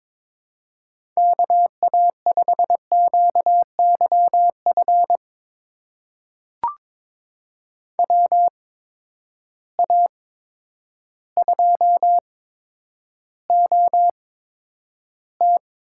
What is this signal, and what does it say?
Morse code, heard as a keyed tone, about 22 wpm, spaced as for 10 wpm, 700 Hz. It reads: KA5QYF WA2OT
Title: USA Call Signs - 6 Characters - 8x Character Spacing - 22wpm